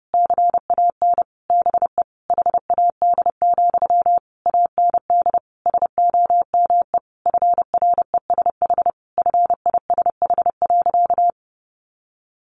Morse code, heard as a keyed tone, about 28 wpm, 700 Hz. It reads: CAN BE HAD, AND SOME FRESH FISH.
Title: Sentences from Top 1500 Words 30wpm